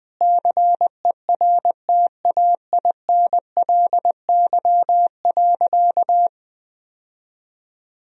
Morse code, heard as a keyed tone, about 21 wpm, 700 Hz. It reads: CERTAINLY.
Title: Sentences from Top 1000 Words - Review Part 1 - 20wpm